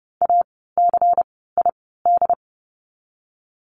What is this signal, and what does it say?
Morse code, heard as a keyed tone, about 30 wpm, 700 Hz. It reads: A C I D
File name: Lesson 16f - Letter c - Words 30wpm